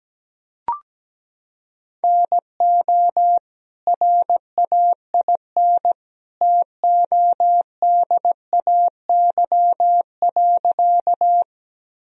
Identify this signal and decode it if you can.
Morse code, heard as a keyed tone, about 17 wpm, 700 Hz. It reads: NO RAIN TODAY.
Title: Sentences from Top 1000 Words - Review Part 1 - 17wpm